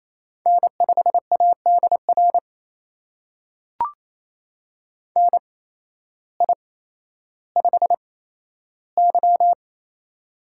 Morse code, heard as a keyed tone, about 27 wpm, spaced as for 13 wpm, 700 Hz. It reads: N5ADR NI5Y
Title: USA Call Signs - 8x Character Spacing - 28wpm